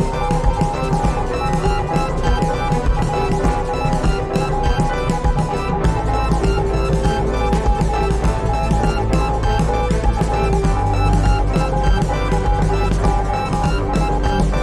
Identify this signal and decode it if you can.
Morse code, heard as a keyed tone, about 20 wpm, 820 Hz. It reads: KNOW. WHERE TO GO FOLLOW